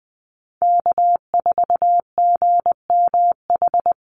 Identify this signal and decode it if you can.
Morse code, heard as a keyed tone, about 20 wpm, 700 Hz. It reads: K4GMH
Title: Call Signs Repeated 2x - 20wpm